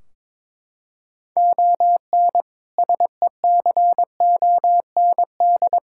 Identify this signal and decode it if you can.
Morse code, heard as a keyed tone, about 22 wpm, 700 Hz. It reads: ON SECOND
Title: Sentences from Top 1600 Words 22wpm